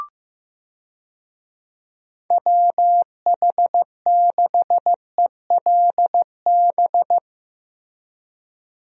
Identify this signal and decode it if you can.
Morse code, heard as a keyed tone, about 15 wpm, 700 Hz. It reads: WH6ELB
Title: USA Call Signs - 6 Characters - Repeated 3x - 15wpm